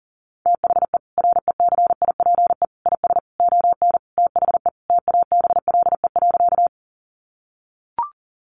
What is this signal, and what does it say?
Morse code, heard as a keyed tone, about 40 wpm, 700 Hz. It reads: THE RECIPE IS ON THE TABLE.
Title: Sentences from Top 1200 Words 40wpm